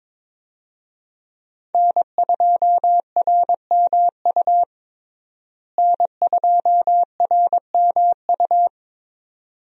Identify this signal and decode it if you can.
Morse code, heard as a keyed tone, about 22 wpm, 700 Hz. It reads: N2RMU N2RMU
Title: USA Call Signs - 5 Characters - Repeated 2x - 22wpm